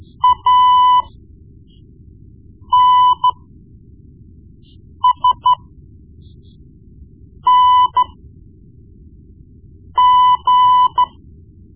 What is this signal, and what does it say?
Morse code, heard as a keyed tone, about 13 wpm, spaced as for 4 wpm, 1000 Hz. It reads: ANSNG
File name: Morse Lessons / Morse Lessons / Tape-1B